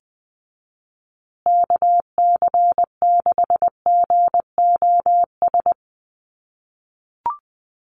Morse code, heard as a keyed tone, about 20 wpm, 700 Hz. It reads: KC6GOS
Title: USA Call Signs Repeated 3x 20wpm